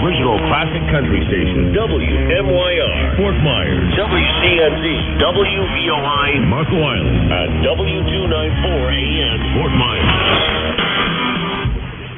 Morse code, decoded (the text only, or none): none